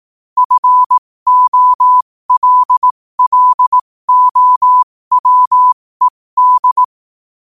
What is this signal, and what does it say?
Morse code, heard as a keyed tone, about 18 wpm, spaced as for 15 wpm, 1000 Hz. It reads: FOLLOWED
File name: qotdfiles / 15wpm